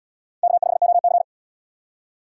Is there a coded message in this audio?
Yes: Morse code from 0.4 to 1.2 s, heard as a keyed tone, about 65 wpm, 700 Hz. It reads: K5OC